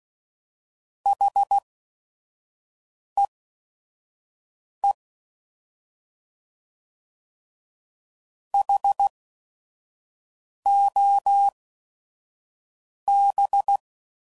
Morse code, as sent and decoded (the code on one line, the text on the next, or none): .... . / .... --- -...
HEE HOB